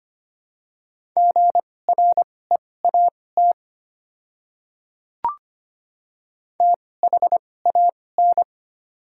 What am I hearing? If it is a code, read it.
Morse code, heard as a keyed tone, about 25 wpm, spaced as for 18 wpm, 700 Hz. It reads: GREAT THAN
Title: 2x Character Spacing - Top 300 Words - 25wpm